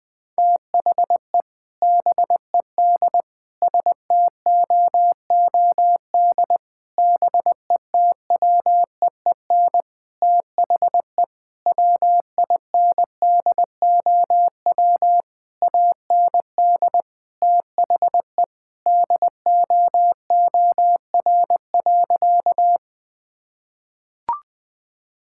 Morse code, heard as a keyed tone, about 20 wpm, 700 Hz. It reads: THE BED STOOD BETWEEN THE WINDOW AND THE DOOR.
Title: Sentences from Top 1600 Words 20wpm